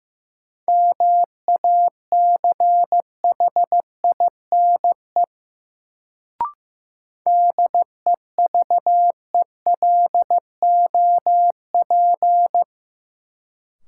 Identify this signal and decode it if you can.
Morse code, heard as a keyed tone, about 15 wpm, 700 Hz. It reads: MACHINE DEVELOP
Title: Sentences from Top 1000 Words - Review Part 2 - New Word Drill - 15wpm